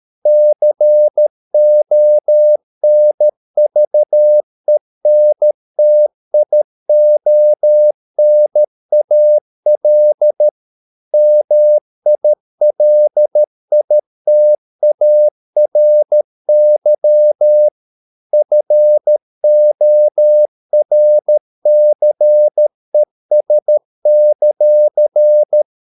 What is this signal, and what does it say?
Morse code, heard as a keyed tone, about 13 wpm, 600 Hz. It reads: CONVENTIONAL MILITARY FORCES;